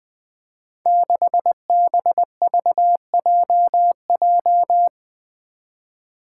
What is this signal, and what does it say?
Morse code, heard as a keyed tone, about 20 wpm, 700 Hz. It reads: 6BVJJ
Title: Five Challenging Letter-Number Combinations 20wpm